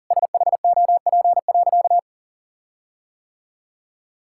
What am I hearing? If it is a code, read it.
Morse code, heard as a keyed tone, about 40 wpm, 700 Hz. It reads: SHOP.
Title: Sentences from Top 1600 Words 40wpm